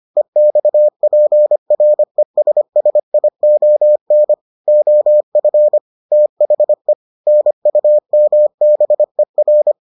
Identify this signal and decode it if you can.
Morse code, heard as a keyed tone, about 25 wpm, 600 Hz. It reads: EXPRESSION OF THE NUMBER